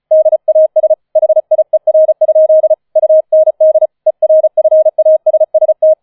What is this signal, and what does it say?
Morse code, heard as a keyed tone, about 34 wpm, 630 Hz. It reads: DAS HIER? UND ERFASST
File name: 005 Fünfter Streich 35wpm